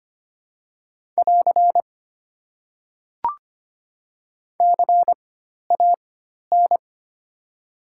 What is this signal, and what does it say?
Morse code, heard as a keyed tone, about 25 wpm, spaced as for 12 wpm, 690 Hz. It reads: <AR> CAN
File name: Top 90 QSO Elements - 4x Character Spacing - 25wpm